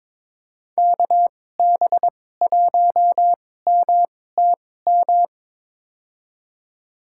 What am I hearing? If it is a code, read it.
Morse code, heard as a keyed tone, about 22 wpm, spaced as for 16 wpm, 700 Hz. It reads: KB1MTM
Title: USA Call Signs - 6 Characters - 2x Character Spacing - 22wpm